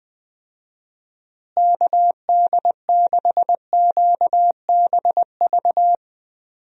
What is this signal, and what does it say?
Morse code, heard as a keyed tone, about 20 wpm, 700 Hz. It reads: KD6QBV
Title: USA Call Signs - 6 Characters - Repeated 2x - 20wpm